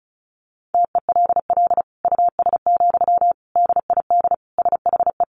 Morse code, heard as a keyed tone, about 34 wpm, 700 Hz. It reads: TELL US, DID SHE
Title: Sentences from Top 1500 Words 35wpm